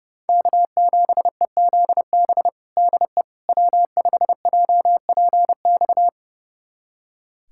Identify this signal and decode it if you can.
Morse code, heard as a keyed tone, about 30 wpm, 700 Hz. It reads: K7EZB DE W5JPX
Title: USA Callsign DE Callsign - 5 Characters - 30wpm